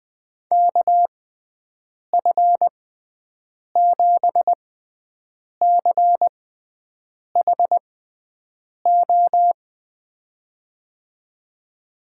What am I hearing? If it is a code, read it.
Morse code, heard as a keyed tone, about 20 wpm, spaced as for 7 wpm, 700 Hz. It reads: KF7CHO
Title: USA Call Signs - 6 Characters - 6x Character Spacing - 20wpm